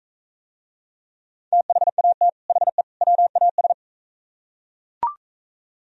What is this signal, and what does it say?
Morse code, heard as a keyed tone, about 44 wpm, 700 Hz. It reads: THAT HE WAS